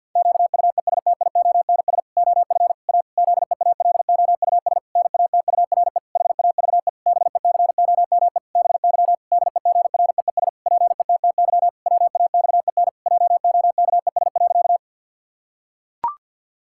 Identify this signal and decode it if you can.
Morse code, heard as a keyed tone, about 52 wpm, 700 Hz. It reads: QUESTIONS OF A ZEALOUS NATURE HAVE BECOME BY DEGREES PETTY WAXEN JOKES.
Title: Pangram Challenge v2 - 50wpm